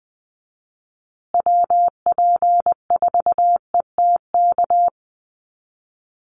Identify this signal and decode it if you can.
Morse code, heard as a keyed tone, about 20 wpm, 700 Hz. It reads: WP4ETK
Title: USA Call Signs Repeated 2x 20wpm